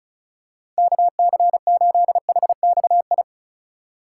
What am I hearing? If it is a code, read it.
Morse code, heard as a keyed tone, about 36 wpm, 700 Hz. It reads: KC8HXI
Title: USA Call Signs - 6 Characters - 4x Character Spacing - 35wpm